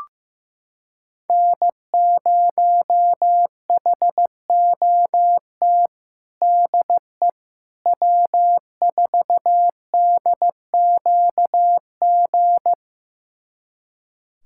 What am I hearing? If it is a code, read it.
Morse code, heard as a keyed tone, about 15 wpm, 700 Hz. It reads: N0HOT DE W4DQG
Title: USA Callsign DE Callsign - 5 Characters - 15wpm